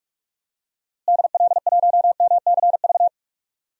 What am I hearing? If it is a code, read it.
Morse code, heard as a keyed tone, about 42 wpm, 700 Hz. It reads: DL1MCV
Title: DXCC-100 Call Signs - 6 Characters - Repeated 3x - 45wpm